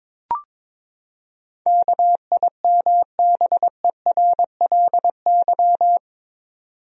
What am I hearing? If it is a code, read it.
Morse code, heard as a keyed tone, about 22 wpm, 700 Hz. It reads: KIMBERLY
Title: USA Common Names - 22wpm